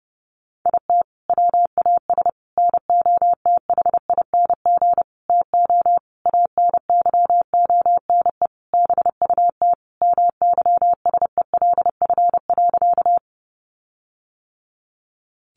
Morse code, heard as a keyed tone, about 30 wpm, 700 Hz. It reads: IT WAS NOTHING TO ANYONE BUT MYSELF.
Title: Sentences from Top 1300 Words 30wpm